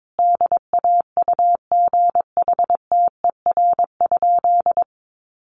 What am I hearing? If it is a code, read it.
Morse code, heard as a keyed tone, about 22 wpm, 700 Hz. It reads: DAUGHTER?
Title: Sentences from Top 1500 Words 22wpm